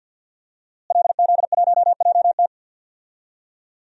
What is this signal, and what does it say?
Morse code, heard as a keyed tone, about 46 wpm, 700 Hz. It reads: LZ1JT
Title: DXCC-100 Call Signs - 5 Characters - Repeated 3x - 50wpm